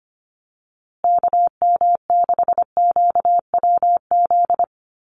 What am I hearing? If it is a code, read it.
Morse code, heard as a keyed tone, about 25 wpm, 700 Hz. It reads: KM6QWZ
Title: USA Call Signs Repeated 2x 25wpm